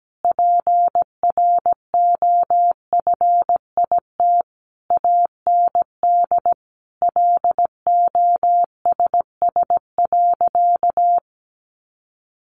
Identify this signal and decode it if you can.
Morse code, heard as a keyed tone, about 17 wpm, 700 Hz. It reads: PROFIT AND LOSS.
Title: Sentences from Top 1400 Words 17wpm